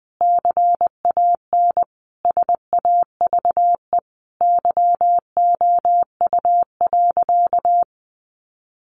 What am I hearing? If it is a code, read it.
Morse code, heard as a keyed tone, about 20 wpm, 700 Hz. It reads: CAN SAVE YOU.
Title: Sentences from Top 1100 Words 20wpm